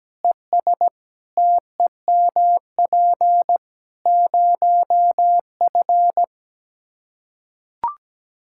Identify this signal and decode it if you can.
Morse code, heard as a keyed tone, about 17 wpm, 700 Hz. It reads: ES TEMP 0F
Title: Practice QSO - Exchange 2 - Info - 17wpm